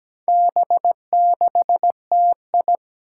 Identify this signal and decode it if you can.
Morse code, heard as a keyed tone, about 17 wpm, 700 Hz. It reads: B6TI
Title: Four Challenging Letter-Number Combinations 17wpm